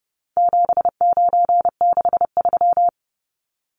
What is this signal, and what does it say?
Morse code, heard as a keyed tone, about 30 wpm, 700 Hz. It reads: 7963
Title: Numbers - 4-Digit 30wpm